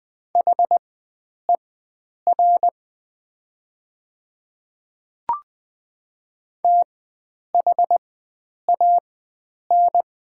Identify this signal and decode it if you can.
Morse code, heard as a keyed tone, about 20 wpm, spaced as for 9 wpm, 690 Hz. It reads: HER THAN
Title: 4x Character Spacing - Top 100 Words - 20wpm